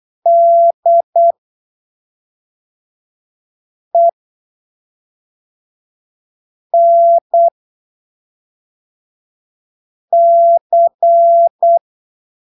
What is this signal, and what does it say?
Morse code, heard as a keyed tone, about 8 wpm, spaced as for 3 wpm, 680 Hz. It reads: DENC
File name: arquivos / texto / texto 28 08ppm